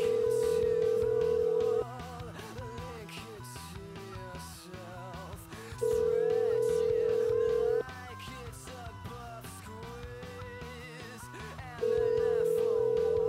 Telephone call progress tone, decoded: ringback tone